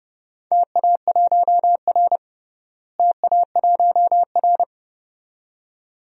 Morse code, heard as a keyed tone, about 30 wpm, 700 Hz. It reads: TA1R TA1R
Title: DXCC-100 Call Signs - 4 Characters - Repeated 2x - 30wpm